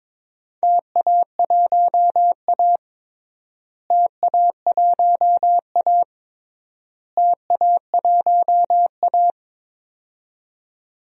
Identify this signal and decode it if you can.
Morse code, heard as a keyed tone, about 21 wpm, 700 Hz. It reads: TA1A TA1A TA1A
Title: DXCC-100 Call Signs - 4 Characters - Repeated 3x - 22wpm